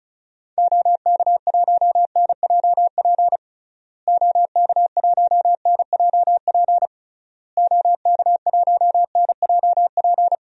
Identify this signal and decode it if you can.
Morse code, heard as a keyed tone, about 36 wpm, 700 Hz. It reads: OK1NJP OK1NJP OK1NJP
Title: DXCC-100 Call Signs - 6 Characters - Repeated 3x - 35wpm